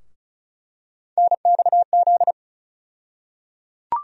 Morse code, heard as a keyed tone, about 34 wpm, 700 Hz. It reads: NXZ
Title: USA Call Sign Postfixes - 3 Letters 35wpm